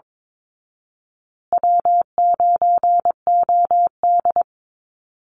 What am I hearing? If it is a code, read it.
Morse code, heard as a keyed tone, about 22 wpm, 700 Hz. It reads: W9OD
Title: Call Signs Repeated 2x - 22wpm